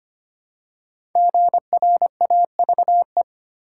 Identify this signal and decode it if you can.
Morse code, heard as a keyed tone, about 25 wpm, 700 Hz. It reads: GRAVE
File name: 5 Letter Words - First 3 Letters Given - 25wpm